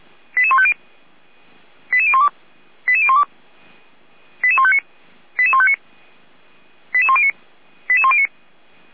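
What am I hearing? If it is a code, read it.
Five-tone selective call (ZVEI), heard as quick runs of five tones, tones of about 70 ms. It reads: 88150 88012 88012 88159 88159 88109 88109
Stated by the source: ZVEI five-tone call received by SDR, from the feuerwehr-orlamuende.org fire brigade audio server.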